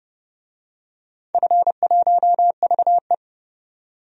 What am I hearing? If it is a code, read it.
Morse code, heard as a keyed tone, about 30 wpm, 700 Hz. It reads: F1VE
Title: DXCC-100 Call Signs - 4 Characters - 30wpm